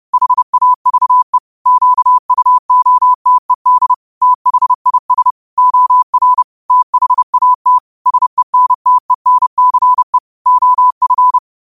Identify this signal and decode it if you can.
Morse code, heard as a keyed tone, about 30 wpm, 1000 Hz. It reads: HAVE QUOTED THIS OR THAT SENTENCE OF
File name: qotdfiles / 30wpm